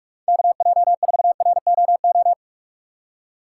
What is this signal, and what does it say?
Morse code, heard as a keyed tone, about 46 wpm, 700 Hz. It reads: KJ4ROO
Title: USA Call Signs - 6 Characters - 6x Character Spacing - 45wpm